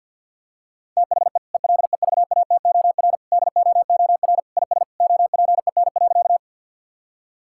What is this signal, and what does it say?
Morse code, heard as a keyed tone, about 50 wpm, 700 Hz. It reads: THE ELEVATOR DOOR IS OPEN.
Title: Sentences from Top 1600 Words 50wpm